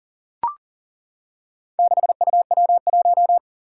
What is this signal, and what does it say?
Morse code, heard as a keyed tone, about 40 wpm, 700 Hz. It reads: 6UW1
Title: Four Challenging Letter-Number Combinations 40wpm